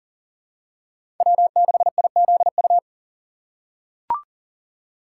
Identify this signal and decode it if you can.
Morse code, heard as a keyed tone, about 40 wpm, 700 Hz. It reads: W6IZU
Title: USA Call Signs - 4x Character Spacing - 40wpm